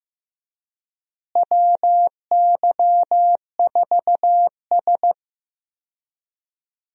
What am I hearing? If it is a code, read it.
Morse code, heard as a keyed tone, about 15 wpm, 700 Hz. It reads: WY4S
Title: USA Call Signs - 4 Characters - Repeated 2x - 15wpm